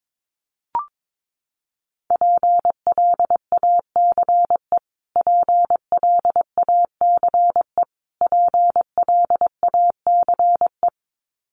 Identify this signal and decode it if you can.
Morse code, heard as a keyed tone, about 22 wpm, 700 Hz. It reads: PLACE PLACE PLACE
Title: Top 300 Words Repeated 22wpm